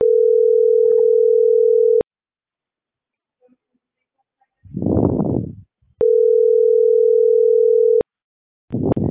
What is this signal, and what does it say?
Ringback tone (North America), 440 + 480 Hz.